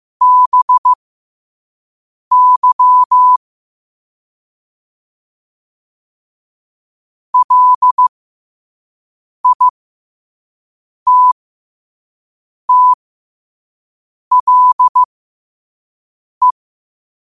Morse code, heard as a keyed tone, about 15 wpm, spaced as for 5 wpm, 1000 Hz. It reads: BY LITTLE
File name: qotdfiles / 5wpm